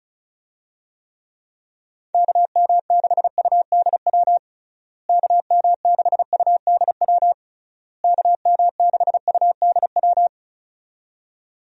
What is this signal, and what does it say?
Morse code, heard as a keyed tone, about 36 wpm, 700 Hz. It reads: KM6UDW KM6UDW KM6UDW
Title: USA Call Signs - 6 Characters - Repeated 3x - 35wpm